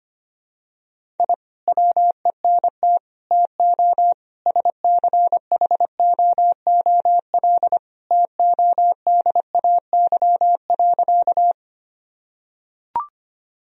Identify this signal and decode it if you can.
Morse code, heard as a keyed tone, about 25 wpm, 700 Hz. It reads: I WENT TO SCHOOL TODAY.
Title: Sentences from Top 1000 Words - Review Part 1 - 25wpm